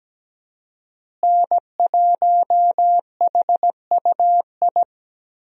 Morse code, heard as a keyed tone, about 17 wpm, 700 Hz. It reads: N1HUI
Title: USA Call Signs - 5 Characters - Repeated 3x - 17wpm